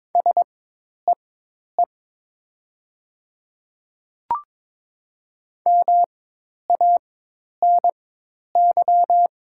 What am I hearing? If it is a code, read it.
Morse code, heard as a keyed tone, about 22 wpm, spaced as for 10 wpm, 700 Hz. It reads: SEE MANY
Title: 4x Character Spacing - Top 100 Words - 22wpm